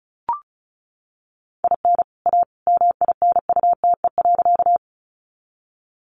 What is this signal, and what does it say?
Morse code, heard as a keyed tone, about 34 wpm, 700 Hz. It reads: IN A MINUTE.
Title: Sentences from Top 500 Words 35wpm